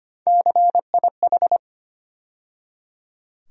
Morse code, heard as a keyed tone, about 25 wpm, 700 Hz. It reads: CIH
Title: USA Call Sign Postfixes - 3 Letters 25wpm